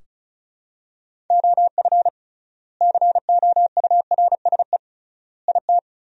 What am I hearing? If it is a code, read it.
Morse code, heard as a keyed tone, about 34 wpm, 700 Hz. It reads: OF COURSE IT
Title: 3x Word-Spacing - Sentences from Top 500 Words - 35wpm